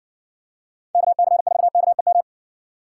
Morse code, heard as a keyed tone, about 59 wpm, 700 Hz. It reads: KC4KEM